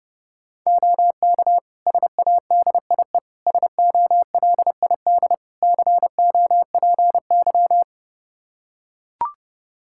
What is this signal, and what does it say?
Morse code, heard as a keyed tone, about 30 wpm, 700 Hz. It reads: OK SADIE SOLID COPY
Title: Practice QSO - Exchange 3 - More Info - 30wpm